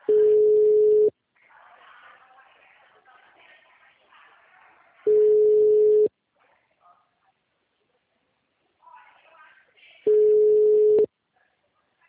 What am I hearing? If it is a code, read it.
Ringback tone (Europe, 425 Hz), with speech.